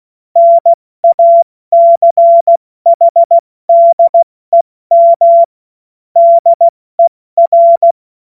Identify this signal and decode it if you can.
Morse code, heard as a keyed tone, about 16 wpm, 680 Hz. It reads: NACHDEM DER
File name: Morsekurs / Witze / Witz 30 Januar 2024 70 F80